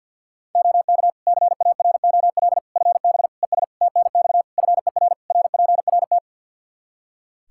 Morse code, heard as a keyed tone, about 51 wpm, 700 Hz. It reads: OK CAROL FB ES TNX FER RPRT
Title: K1USN SST QSO Practice 50wpm